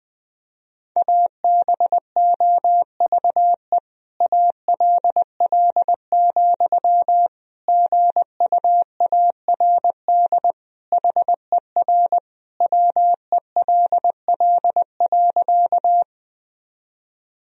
Morse code, heard as a keyed tone, about 20 wpm, 700 Hz. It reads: ABOVE ALL, GUARD HER WELL.
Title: Sentences from Top 1000 Words - Review Part 4 - 20wpm